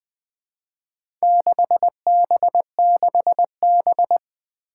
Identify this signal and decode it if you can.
Morse code, heard as a keyed tone, about 20 wpm, 700 Hz. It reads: 6B6B
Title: Four Challenging Letter-Number Combinations 20wpm